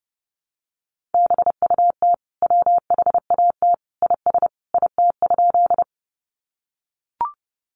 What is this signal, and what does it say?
Morse code, heard as a keyed tone, about 30 wpm, 700 Hz. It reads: BUT WHAT IS IT?
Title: Sentences from Top 100 Words 30wpm